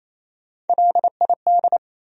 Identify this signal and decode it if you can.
Morse code, heard as a keyed tone, about 37 wpm, 700 Hz. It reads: LID